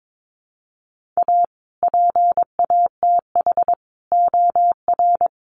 Morse code, heard as a keyed tone, about 22 wpm, 700 Hz. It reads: A PATH OR
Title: Sentences from Top 1400 Words 22wpm